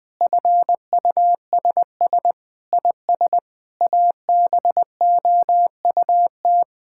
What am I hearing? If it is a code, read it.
Morse code, heard as a keyed tone, about 21 wpm, 700 Hz. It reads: FUSS IS ABOUT
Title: Sets of 8 Words - 20wpm